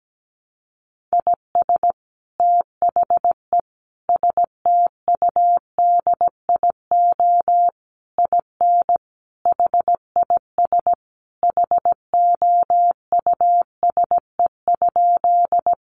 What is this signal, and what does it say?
Morse code, heard as a keyed tone, about 17 wpm, 700 Hz. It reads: IS THE STUDIO IN HIS HOUSE?
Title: Sentences from Top 1200 Words 17wpm